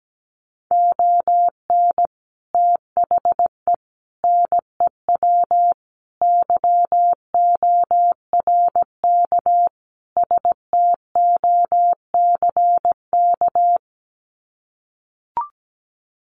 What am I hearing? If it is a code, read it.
Morse code, heard as a keyed tone, about 17 wpm, 700 Hz. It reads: ON THE NEW YORK STOCK
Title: Sets of 5 Words Encore 17wpm